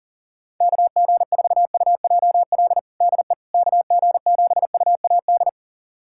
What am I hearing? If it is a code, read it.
Morse code, heard as a keyed tone, about 40 wpm, 700 Hz. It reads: KG4UJL DE KG7UAD
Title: USA Callsign DE Callsign 40wpm